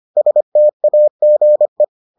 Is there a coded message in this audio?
Yes: Morse code from 0.2 to 1.8 s, heard as a keyed tone, about 25 wpm, 600 Hz. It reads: STAGE